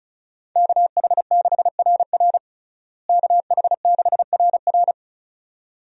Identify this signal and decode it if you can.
Morse code, heard as a keyed tone, about 36 wpm, 700 Hz. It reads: KH6RR KH6RR
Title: USA Call Signs - 5 Characters - Repeated 2x - 35wpm